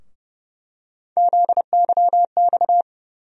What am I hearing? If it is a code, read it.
Morse code, heard as a keyed tone, about 30 wpm, 700 Hz. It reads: ZYX
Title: USA Call Sign Postfixes - 3 Letters 30wpm